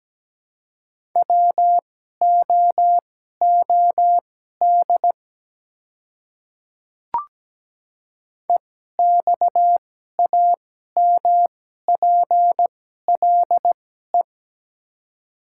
Morse code, heard as a keyed tone, about 17 wpm, spaced as for 12 wpm, 700 Hz. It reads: WOOD EXAMPLE
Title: 2x Character Spacing - Top 300 Words - 17wpm